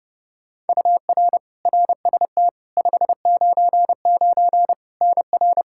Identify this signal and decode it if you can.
Morse code, heard as a keyed tone, about 31 wpm, 700 Hz. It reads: UR RST 599 NR
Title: Practice QSO - Exchange 1 - Beginning - 30wpm